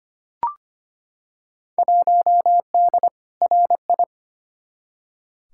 Morse code, heard as a keyed tone, about 25 wpm, 700 Hz. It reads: 1D RI
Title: Field Day Exchanges - 25wpm